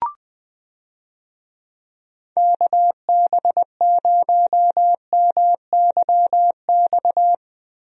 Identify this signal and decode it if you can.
Morse code, heard as a keyed tone, about 21 wpm, 700 Hz. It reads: KB0MYX
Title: USA Call Signs - 6 Characters - Repeated 3x - 20wpm